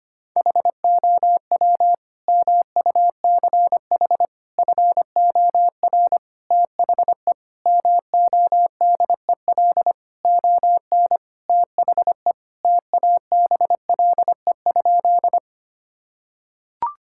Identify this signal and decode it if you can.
Morse code, heard as a keyed tone, about 25 wpm, 700 Hz. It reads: HOW MUCH FOR THE MODEL ON THE TABLE?
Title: Sentences from Top 1000 Words - Review Part 5 - 25wpm